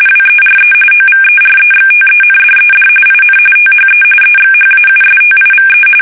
RTTY, about 45.45 baud. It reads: WERE OVEREXTENDED IN THE EXHAUSTIVE
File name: fsk850